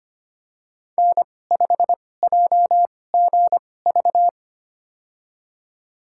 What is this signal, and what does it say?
Morse code, heard as a keyed tone, about 25 wpm, spaced as for 18 wpm, 700 Hz. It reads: N5JGV
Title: USA Call Signs - 2x Character Spacing - 25wpm